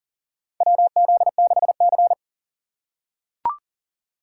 Morse code, heard as a keyed tone, about 40 wpm, 700 Hz. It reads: WZ6C
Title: USA Call Signs - 4 Characters - 3x Character Spacing - 40wpm